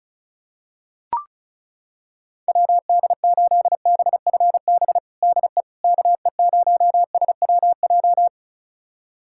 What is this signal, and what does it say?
Morse code, heard as a keyed tone, about 36 wpm, 700 Hz. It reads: WD8BFB DE KE0SWJ
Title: USA Callsign DE Callsign - 6 Characters - 35wpm